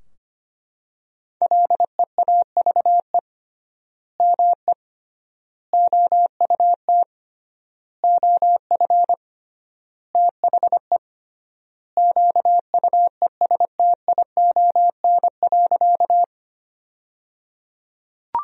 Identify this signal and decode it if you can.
Morse code, heard as a keyed tone, about 25 wpm, 700 Hz. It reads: LEAVE ME OUT OF THE QUESTION.
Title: 3x Word-Spacing - Sentences from Top 400 Words - 25wpm